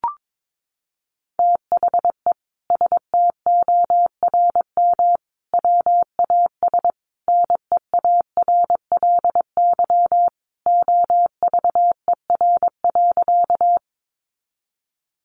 Morse code, heard as a keyed tone, about 22 wpm, 700 Hz. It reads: THE STORM WAS NEARLY OVER.
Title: Sentences from Top 1500 Words 22wpm